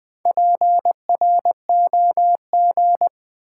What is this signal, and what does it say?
Morse code, heard as a keyed tone, about 20 wpm, 700 Hz. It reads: PROG
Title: Build A Word - 2 Letters At A Time - 8-Letter Words - 20wpm